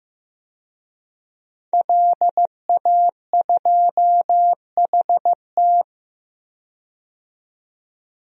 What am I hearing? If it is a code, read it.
Morse code, heard as a keyed tone, about 15 wpm, 700 Hz. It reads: LA2HT
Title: DXCC-100 Call Signs - 5 Characters - 15wpm